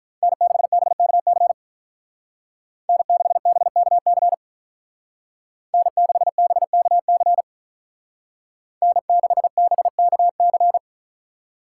Morse code, heard as a keyed tone, about 46 wpm, 700 Hz. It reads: N6BKC N6BKC N6BKC N6BKC